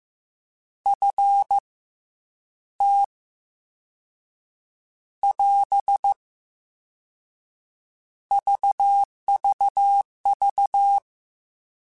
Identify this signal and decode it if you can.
Morse code, heard as a keyed tone, about 15 wpm, 800 Hz. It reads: F T <AS> VVV
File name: Misc / Misc-mp3s / Jumps @13